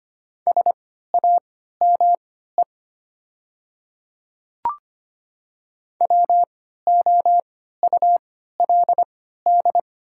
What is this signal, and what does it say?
Morse code, heard as a keyed tone, about 25 wpm, spaced as for 14 wpm, 700 Hz. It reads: SAME WOULD
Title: 3x Character Spacing - Top 200 Words - 25wpm